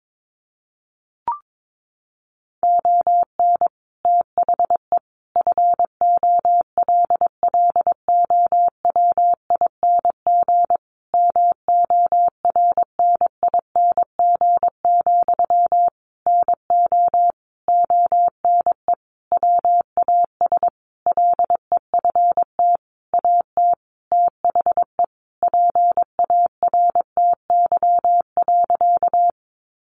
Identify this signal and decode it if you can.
Morse code, heard as a keyed tone, about 22 wpm, 700 Hz. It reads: ON THE FOLLOWING MORNING, NO ONE WAS LEFT AT THE PARTY.
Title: Sentences from Top 1400 Words 22wpm